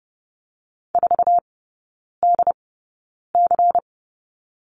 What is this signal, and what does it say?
Morse code, heard as a keyed tone, about 30 wpm, spaced as for 9 wpm, 700 Hz. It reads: 4DC
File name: Lesson 16e - Letter c - Three Characters 30wpm